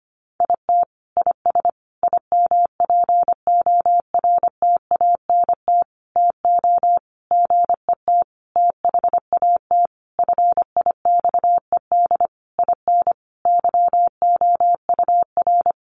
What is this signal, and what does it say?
Morse code, heard as a keyed tone, about 25 wpm, 700 Hz. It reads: IT IS IMPORTANT TO GET THAT FIXED IN YOUR
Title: Sentences from Top 1100 Words 25wpm